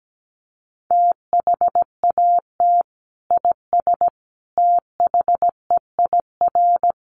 Morse code, heard as a keyed tone, about 17 wpm, 700 Hz. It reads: THAT IS THEIR
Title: Sentences from Top 1300 Words 17wpm